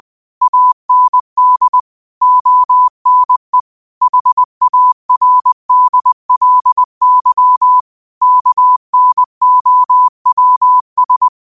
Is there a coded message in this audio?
Yes: Morse code from 0.4 to 11.3 s, heard as a keyed tone, about 20 wpm, 1000 Hz. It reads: AND ONE HARDLY KNOWS